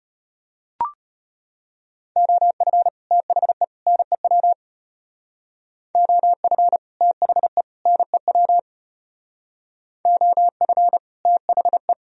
Morse code, heard as a keyed tone, about 30 wpm, 700 Hz. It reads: OF THE NEW OF THE NEW OF THE
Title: Speed-Racing - Sets of 3 Words 25wpm